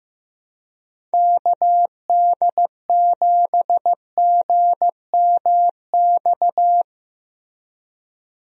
Morse code, heard as a keyed tone, about 15 wpm, 700 Hz. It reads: KD7GMX
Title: USA Call Signs - 6 Characters - Repeated 3x - 15wpm